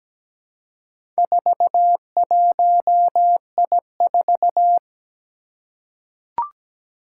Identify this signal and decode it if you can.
Morse code, heard as a keyed tone, about 17 wpm, 700 Hz. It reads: 41I4
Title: Four Challenging Letter-Number Combinations 17wpm